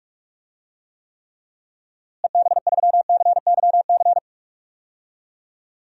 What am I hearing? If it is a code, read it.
Morse code, heard as a keyed tone, about 46 wpm, 700 Hz. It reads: EB3CYC
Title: DXCC-100 Call Signs - 6 Characters - Repeated 3x - 45wpm